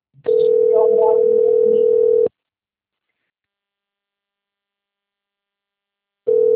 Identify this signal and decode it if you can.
Ringback tone (North America), 440 + 480 Hz.
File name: rg-1600-unknown- - - .21647